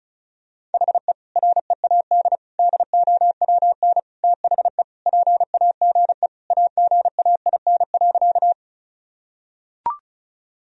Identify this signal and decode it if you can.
Morse code, heard as a keyed tone, about 35 wpm, 700 Hz. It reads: HE READ DOWN THE PAGE AGAIN.
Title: Sentences from Top 1000 Words - Review Part 1 - 35wpm